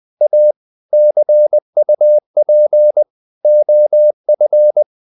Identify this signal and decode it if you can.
Morse code, heard as a keyed tone, about 20 wpm, 600 Hz. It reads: A CUP OF
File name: the old man and the sea / the old man and the sea part 1 20wpm